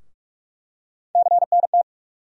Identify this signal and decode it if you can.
Morse code, heard as a keyed tone, about 46 wpm, 700 Hz. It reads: CNT
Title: USA Call Sign Postfixes - 3 Letters 45wpm